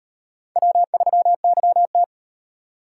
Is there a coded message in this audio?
Yes: Morse code from 0.6 to 2.0 s, heard as a keyed tone, about 38 wpm, 700 Hz. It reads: W3YT